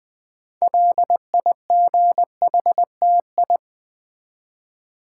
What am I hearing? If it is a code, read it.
Morse code, heard as a keyed tone, about 20 wpm, 700 Hz. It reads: LIGHTI
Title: Build A Word - 3 Letters At A Time - 8-Letter Words - 20wpm